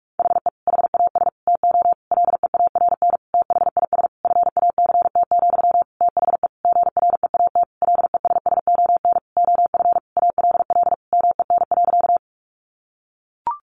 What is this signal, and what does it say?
Morse code, heard as a keyed tone, about 44 wpm, 700 Hz. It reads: HE HAS TO LEARN THIS FACT, THE GREAT LESSON OF ALL MEN.
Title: Sentences from Top 1200 Words 45wpm